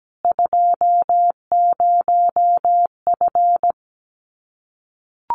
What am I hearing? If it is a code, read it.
Morse code, heard as a keyed tone, about 17 wpm, 700 Hz. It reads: 20F
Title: Common QSO Phrases Encore 17wpm